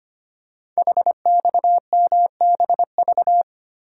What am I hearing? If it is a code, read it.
Morse code, heard as a keyed tone, about 25 wpm, 700 Hz. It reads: HXMBV
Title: Code Groups - 5 Letters - 25wpm